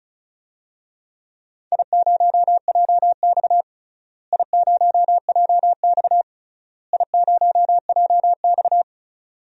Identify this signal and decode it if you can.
Morse code, heard as a keyed tone, about 34 wpm, 700 Hz. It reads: I0JX I0JX I0JX
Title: DXCC-100 Call Signs - 4 Characters - Repeated 3x - 35wpm